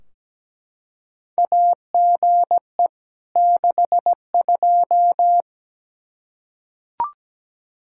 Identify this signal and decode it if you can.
Morse code, heard as a keyed tone, about 17 wpm, 700 Hz. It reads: AGE 62
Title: K1USN SST QSO Practice 17wpm